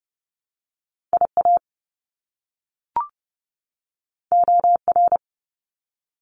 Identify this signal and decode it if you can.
Morse code, heard as a keyed tone, about 30 wpm, 700 Hz. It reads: IA OR
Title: US States 30wpm